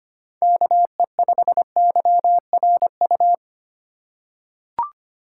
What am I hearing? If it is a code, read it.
Morse code, heard as a keyed tone, about 25 wpm, 690 Hz. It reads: KE5YRU
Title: USA Call Signs - 6 Characters - 3x Character Spacing - 25wpm